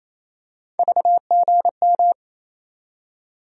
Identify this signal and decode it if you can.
Morse code, heard as a keyed tone, about 28 wpm, 700 Hz. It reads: VGM
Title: USA Call Sign Postfixes - 3 Letters 28wpm